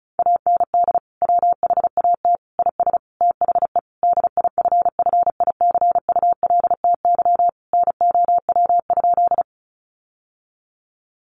Morse code, heard as a keyed tone, about 34 wpm, 700 Hz. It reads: AND WHAT IS THE DIFFICULTY NOW?
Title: Sentences from Top 1500 Words 35wpm